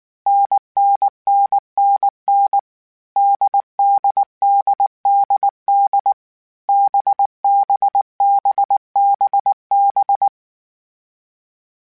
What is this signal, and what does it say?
Morse code, heard as a keyed tone, about 19 wpm, 800 Hz. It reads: NNNNN DDDDD BBBBB
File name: Difficult Combos / Difficult Combos-mp3s / NDB NDB6 @18